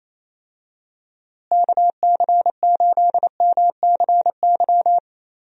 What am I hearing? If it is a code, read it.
Morse code, heard as a keyed tone, about 28 wpm, 700 Hz. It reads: KC8MCY
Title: USA Call Signs - 6 Characters - Repeated 3x - 28wpm